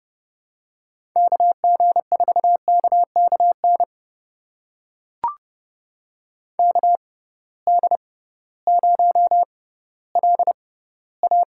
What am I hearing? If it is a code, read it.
Morse code, heard as a keyed tone, about 30 wpm, spaced as for 16 wpm, 700 Hz. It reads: KG4KKN KD0LA
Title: USA Call Signs - 6 Characters - 6x Character Spacing - 30wpm